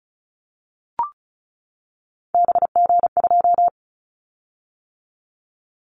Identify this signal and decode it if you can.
Morse code, heard as a keyed tone, about 34 wpm, 700 Hz. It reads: BG2
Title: Callsign Prefixes for Top 100 DXCC - 35wpm